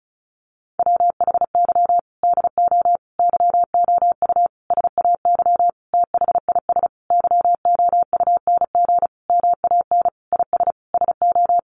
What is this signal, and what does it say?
Morse code, heard as a keyed tone, about 34 wpm, 700 Hz. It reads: WHY DO YOU SAY THIS YOUNG MAN IS SO
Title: Sentences from Top 1100 Words 35wpm